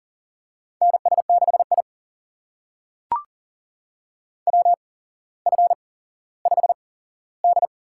Morse code, heard as a keyed tone, about 40 wpm, spaced as for 17 wpm, 700 Hz. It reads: NS6I WF5D
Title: USA Call Signs - 4 Characters - 8x Character Spacing - 40wpm